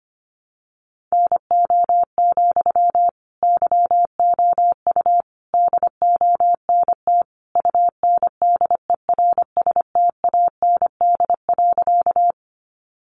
Morse code, heard as a keyed tone, about 25 wpm, 700 Hz. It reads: NO, YOU DONT UNDERSTAND.
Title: Sentences from Top 600 Words 25wpm